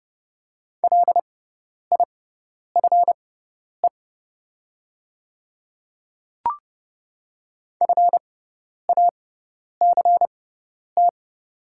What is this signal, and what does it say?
Morse code, heard as a keyed tone, about 30 wpm, spaced as for 10 wpm, 690 Hz. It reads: LIFE FACT
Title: 6x Character Spacing - Top 500 Words - 30wpm